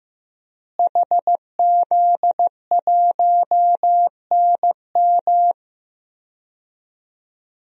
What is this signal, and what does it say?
Morse code, heard as a keyed tone, about 15 wpm, 700 Hz. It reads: HZ1NM
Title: DXCC-100 Call Signs - 5 Characters - Repeated 2x - 15wpm